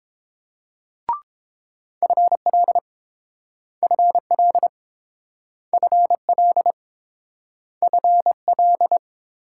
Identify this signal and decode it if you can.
Morse code, heard as a keyed tone, about 30 wpm, 700 Hz. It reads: FL FL FL FL